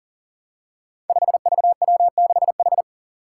Morse code, heard as a keyed tone, about 37 wpm, 700 Hz. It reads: 5VW6H
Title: Five Challenging Letter-Number Combinations 40wpm